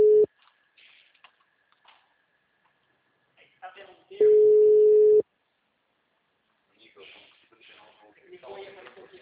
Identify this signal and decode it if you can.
Ringback tone (Europe, 425 Hz), with speech.
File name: out- -2008- - - .30093